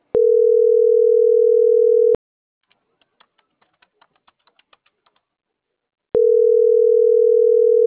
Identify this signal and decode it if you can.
Ringback tone (North America), 440 + 480 Hz.